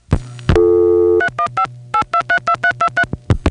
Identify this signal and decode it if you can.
DTMF, heard as telephone touch tones, tones of about 90 ms. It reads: 3158232323